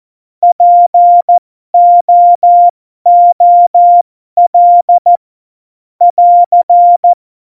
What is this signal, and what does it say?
Morse code, heard as a keyed tone, about 14 wpm, spaced as for 12 wpm, 700 Hz. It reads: POOL <AR>
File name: Morsekurs / Witze